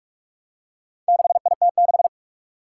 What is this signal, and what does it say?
Morse code, heard as a keyed tone, about 44 wpm, 700 Hz. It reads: 6IT6
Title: Four Challenging Letter-Number Combinations 45wpm